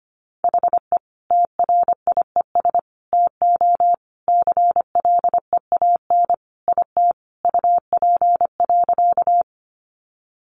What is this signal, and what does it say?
Morse code, heard as a keyed tone, about 25 wpm, 700 Hz. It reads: HE TRIES TO CLEAN IT UP.
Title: Additional Easy Phrases 25wpm